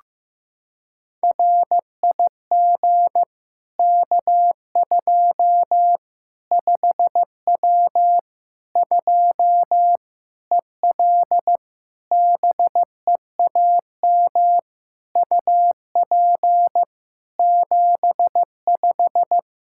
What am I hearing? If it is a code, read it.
Morse code, heard as a keyed tone, about 15 wpm, 700 Hz. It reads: RIG K2 5W 2 EL BEAM UP 75
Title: Practice QSO - Exchange 2 - Info - 15wpm